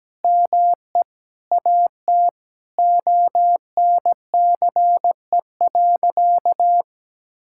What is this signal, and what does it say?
Morse code, heard as a keyed tone, about 17 wpm, 700 Hz. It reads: ME AT ONCE.
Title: Sentences from Top 1000 Words - Review Part 6 - 17wpm